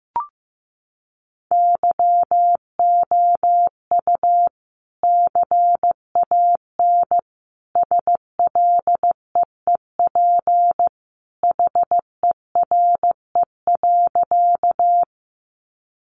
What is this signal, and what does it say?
Morse code, heard as a keyed tone, about 15 wpm, 700 Hz. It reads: YOU CAN SLEEP HERE.
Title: Sentences from Top 500 Words 15wpm